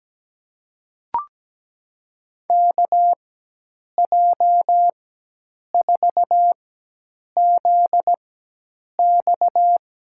Morse code, heard as a keyed tone, about 17 wpm, spaced as for 8 wpm, 700 Hz. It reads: KJ4ZX
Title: USA Call Signs - 4x Character Spacing - 17wpm